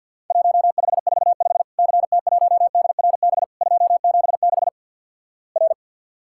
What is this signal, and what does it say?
Morse code, heard as a keyed tone, about 50 wpm, 700 Hz. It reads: 1545 CT1DRB 276 R
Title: Worked All Europe DX Contest - QTC Exchange with Repeats - 50wpm